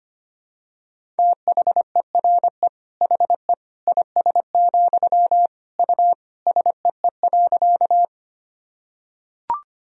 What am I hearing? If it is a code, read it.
Morse code, heard as a keyed tone, about 25 wpm, 700 Hz. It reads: THERE HE IS, U SEE.
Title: Sentences from Top 100 Words and Abbreviations 25wpm